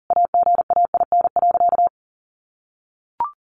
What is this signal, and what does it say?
Morse code, heard as a keyed tone, about 40 wpm, 700 Hz. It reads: AGAIN.
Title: Sentences from Top 1300 Words 40wpm